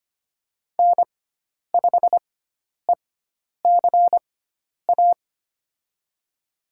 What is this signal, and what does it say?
Morse code, heard as a keyed tone, about 25 wpm, spaced as for 10 wpm, 700 Hz. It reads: N5ECA
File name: Lesson 16g - Letter c - Call Signs 25wpm